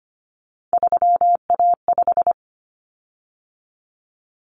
Morse code, heard as a keyed tone, about 25 wpm, 700 Hz. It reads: 3A5
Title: Common Cut Numbers - 3-Digit 25wpm